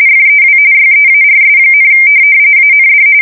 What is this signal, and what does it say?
RTTY, about 45.45 baud. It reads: FEW YEARS, AND
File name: fsk170 / morkrum